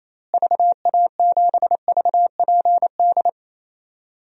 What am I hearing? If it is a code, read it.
Morse code, heard as a keyed tone, about 28 wpm, 700 Hz. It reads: VA7VPD
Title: DXCC-100 Call Signs - 6 Characters - Repeated 3x - 28wpm